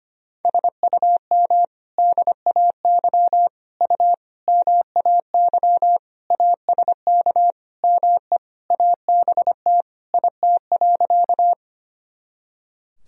Morse code, heard as a keyed tone, about 25 wpm, 700 Hz. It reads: SUM DAY U MAY ASK ME ABT IT.
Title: Sentences from Top 200 Words and Abbreviations 25wpm